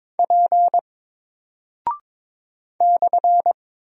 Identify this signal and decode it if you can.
Morse code, heard as a keyed tone, about 21 wpm, 700 Hz. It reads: P /
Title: Single Character (Letter:Number:Punctuation:Prosign) Alt 22wpm